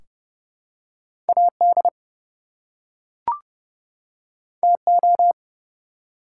Morse code, heard as a keyed tone, about 30 wpm, 700 Hz. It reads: AD TO
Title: Two Letter Words - 30wpm